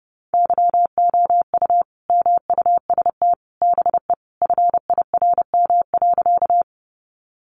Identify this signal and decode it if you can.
Morse code, heard as a keyed tone, about 30 wpm, 700 Hz. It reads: YOU MUST BE FIRM.
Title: Sentences from Top 1400 Words 30wpm